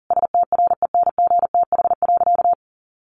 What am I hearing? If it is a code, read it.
Morse code, heard as a keyed tone, about 40 wpm, 700 Hz. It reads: STRENGTH.
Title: Sentences from Top 1300 Words 40wpm